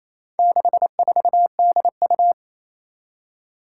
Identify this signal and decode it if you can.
Morse code, heard as a keyed tone, about 27 wpm, 700 Hz. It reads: 64DU
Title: Four Challenging Letter-Number Combinations 28wpm